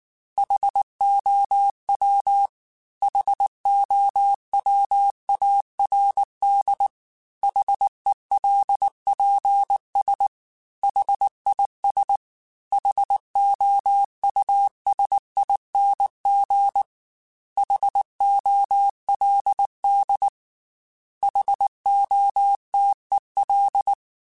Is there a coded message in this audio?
Yes: Morse code from 0.4 to 23.9 s, heard as a keyed tone, about 19 wpm, 800 Hz. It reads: HOW HOWARD HELPS HIS HOUSING HOLD HOTEL